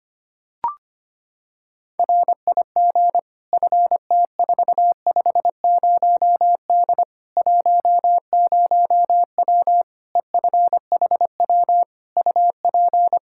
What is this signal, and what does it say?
Morse code, heard as a keyed tone, about 25 wpm, 700 Hz. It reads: RIG FT450D 10W EFHW UP
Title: K1USN SST QSO Practice 25wpm